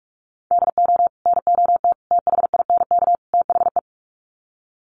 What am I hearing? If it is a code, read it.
Morse code, heard as a keyed tone, about 44 wpm, 700 Hz. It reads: DO NOT THINK THE
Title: Sets of 4 Words 45wpm